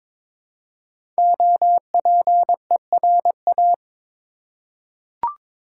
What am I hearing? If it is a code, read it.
Morse code, heard as a keyed tone, about 22 wpm, 700 Hz. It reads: OPERA
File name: 5 Letter Words - First 3 Letters Given - 22wpm